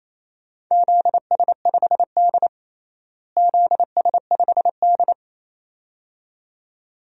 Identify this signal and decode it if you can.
Morse code, heard as a keyed tone, about 28 wpm, 700 Hz. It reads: ZS5D ZS5D
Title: DXCC-100 Call Signs - 4 Characters - Repeated 2x - 28wpm